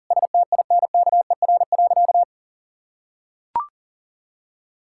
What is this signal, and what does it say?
Morse code, heard as a keyed tone, about 40 wpm, 700 Hz. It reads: STINKER.
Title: Pangram Challenge v2 - 40wpm